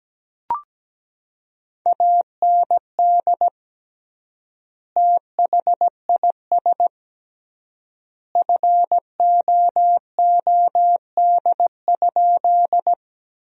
Morse code, heard as a keyed tone, about 17 wpm, 700 Hz. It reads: AND THIS FOOD?
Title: 3x Word-Spacing - Sentences from Top 300 Words - 17wpm